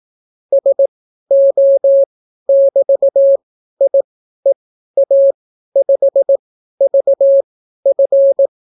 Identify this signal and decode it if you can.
Morse code, heard as a keyed tone, about 18 wpm, spaced as for 12 wpm, 550 Hz. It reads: SO<BT>IEA5VF